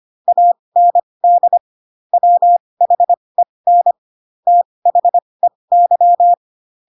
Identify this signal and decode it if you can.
Morse code, heard as a keyed tone, about 25 wpm, spaced as for 20 wpm, 700 Hz. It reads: AND WHEN THEY
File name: Pinocchio / Pinocchio ch4 15 25 ik1hge